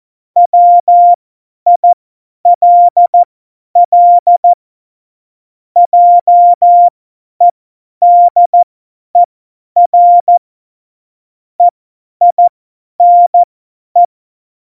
Morse code, heard as a keyed tone, about 14 wpm, spaced as for 10 wpm, 700 Hz. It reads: WILL JEDER EINE
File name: Morsekurs / Witze / Witz 25. Juni 2019, 50 F70